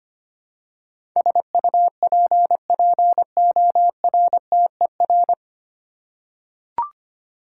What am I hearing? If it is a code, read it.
Morse code, heard as a keyed tone, about 25 wpm, 700 Hz. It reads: SUPPORTER
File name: Nine Letter Words - 25wpm